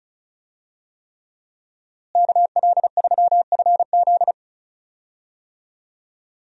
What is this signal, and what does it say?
Morse code, heard as a keyed tone, about 36 wpm, 700 Hz. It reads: KL3FZ
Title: USA Call Signs - 5 Characters - Repeated 3x - 35wpm